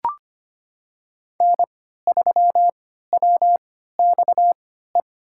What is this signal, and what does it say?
Morse code, heard as a keyed tone, about 25 wpm, spaced as for 14 wpm, 700 Hz. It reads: N3WXE
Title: USA Call Signs - 5 Characters - 3x Character Spacing - 25wpm